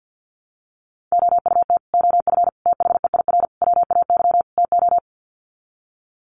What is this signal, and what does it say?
Morse code, heard as a keyed tone, about 50 wpm, 700 Hz. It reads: OUT OF THEIR WAY TO
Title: Sets of 5 Words Encore2 50wpm